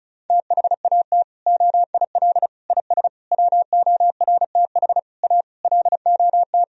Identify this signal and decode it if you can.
Morse code, heard as a keyed tone, about 34 wpm, 700 Hz. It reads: THAT OIL IS WORTH A LOT
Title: Sentences from Top 1000 Words - Review Part 4 - 35wpm